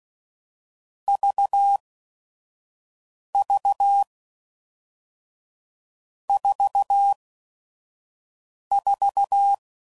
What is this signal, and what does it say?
Morse code, heard as a keyed tone, about 16 wpm, spaced as for 5 wpm, 800 Hz. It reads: VV 44
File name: Combinations / Difficult Combos / AUV AUV4